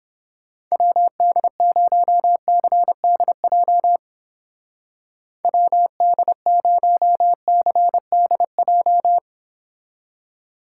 Morse code, heard as a keyed tone, about 31 wpm, 700 Hz. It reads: WD0CDJ WD0CDJ